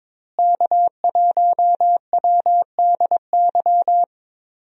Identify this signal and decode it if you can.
Morse code, heard as a keyed tone, about 22 wpm, 700 Hz. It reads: K1WDY
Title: USA Call Signs - 5 Characters - Repeated 2x - 22wpm